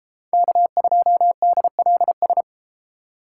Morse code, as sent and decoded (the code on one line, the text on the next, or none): -.- ..--- -.. .-.. ...
K2DLS